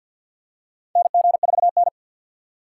Morse code, heard as a keyed tone, about 50 wpm, 700 Hz. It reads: NG4N
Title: USA Call Signs - 4 Characters - 3x Character Spacing - 50wpm